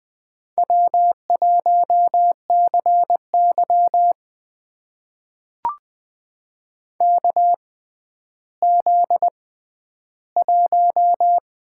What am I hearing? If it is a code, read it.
Morse code, heard as a keyed tone, about 20 wpm, spaced as for 11 wpm, 700 Hz. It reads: W1CY KZ1
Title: USA Call Signs - 4 Characters - 6x Character Spacing - 20wpm